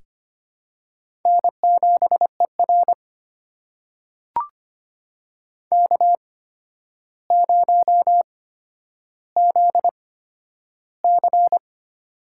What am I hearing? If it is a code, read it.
Morse code, heard as a keyed tone, about 25 wpm, spaced as for 11 wpm, 700 Hz. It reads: N7ER K0ZC
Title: USA Call Signs - 4 Characters - 8x Character Spacing - 25wpm